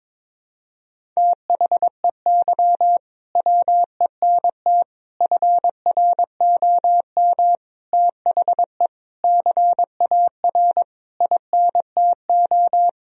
Morse code, heard as a keyed tone, about 22 wpm, 700 Hz. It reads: THEY WENT FROM THE CAR INTO
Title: Sentences from Top 1000 Words - Review Part 1 - 22wpm